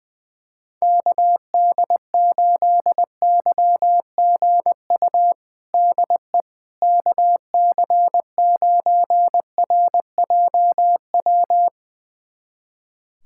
Morse code, heard as a keyed tone, about 21 wpm, 700 Hz. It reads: KD8YGU DE KC9RJW
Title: USA Callsign DE Callsign - 6 Characters - 20wpm